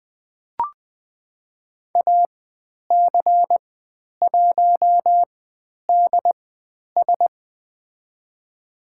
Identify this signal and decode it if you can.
Morse code, heard as a keyed tone, about 20 wpm, spaced as for 10 wpm, 700 Hz. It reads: AC1DS
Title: 109 Lesson 18g - Letter m - Call Signs 20wpm